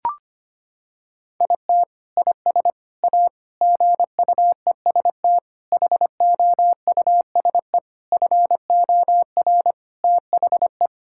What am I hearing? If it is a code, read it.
Morse code, heard as a keyed tone, about 25 wpm, 700 Hz. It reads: IT IS A GUEST HOUSE FOR THE
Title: Sentences from Top 1600 Words 25wpm